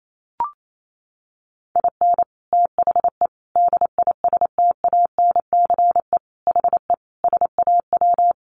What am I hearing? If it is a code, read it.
Morse code, heard as a keyed tone, about 28 wpm, 700 Hz. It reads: IN THE DISTANCE HE SAW
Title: Sentences from Top 1200 Words 28wpm